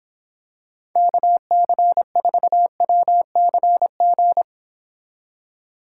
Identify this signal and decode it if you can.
Morse code, heard as a keyed tone, about 26 wpm, 700 Hz. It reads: KC4WCG